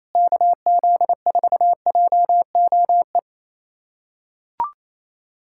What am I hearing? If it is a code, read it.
Morse code, heard as a keyed tone, about 28 wpm, 700 Hz. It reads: KZ4JOE
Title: USA Callsign DE Callsign - 6 Characters - 28wpm